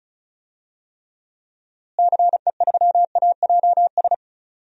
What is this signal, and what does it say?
Morse code, heard as a keyed tone, about 36 wpm, 700 Hz. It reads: CE3AJS
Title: DXCC-100 Call Signs - 6 Characters - Repeated 3x - 35wpm